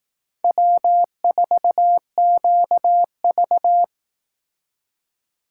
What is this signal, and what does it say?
Morse code, heard as a keyed tone, about 18 wpm, 700 Hz. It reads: W4QV
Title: Speed Racing - USA 1x2 Call Signs 15wpm